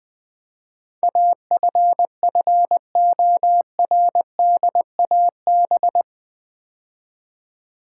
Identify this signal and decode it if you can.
Morse code, heard as a keyed tone, about 20 wpm, 700 Hz. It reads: AFFORDAB
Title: Build A Word - 4 Letters At A Time - 10-Letter Words - 20wpm